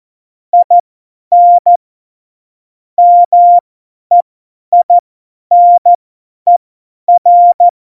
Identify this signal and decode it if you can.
Morse code, heard as a keyed tone, about 14 wpm, spaced as for 10 wpm, 700 Hz. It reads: IN MEINER
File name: Morsekurs / Witze / Witz 25. Juni 2019, 50 F70